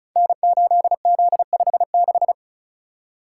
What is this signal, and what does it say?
Morse code, heard as a keyed tone, about 36 wpm, 690 Hz. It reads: N8Z56
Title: Code Groups - 5 Letter:Numbers - 35wpm